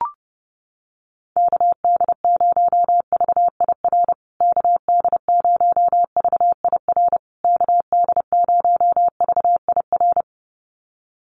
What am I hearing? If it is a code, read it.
Morse code, heard as a keyed tone, about 30 wpm, 700 Hz. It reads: KD0VIR KD0VIR KD0VIR
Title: USA Call Signs Repeated 3x 30wpm